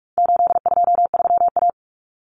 Morse code, heard as a keyed tone, about 44 wpm, 700 Hz. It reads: 823A
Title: Common Cut Numbers - 4-Digit 45wpm